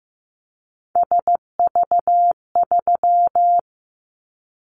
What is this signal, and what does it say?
Morse code, heard as a keyed tone, about 15 wpm, 700 Hz. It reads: SV3
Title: Callsign Prefixes for Top 50 DXCC - 15wpm